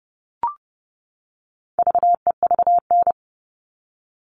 Morse code, heard as a keyed tone, about 30 wpm, 700 Hz. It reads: VEVN
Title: All Cut Numbers - 4-Digit 30wpm